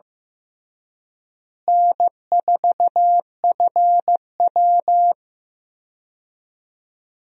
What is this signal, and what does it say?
Morse code, heard as a keyed tone, about 15 wpm, 700 Hz. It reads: N4FW
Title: USA Call Signs - 4 Characters - Repeated 3x - 15wpm